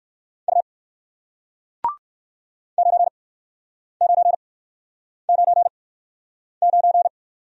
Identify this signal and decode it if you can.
Morse code, heard as a keyed tone, about 61 wpm, spaced as for 9 wpm, 700 Hz. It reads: U 9999